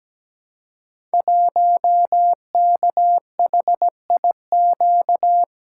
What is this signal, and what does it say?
Morse code, heard as a keyed tone, about 17 wpm, 700 Hz. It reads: 1KHIQ
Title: Code Groups - 5 Letter:Numbers - 17wpm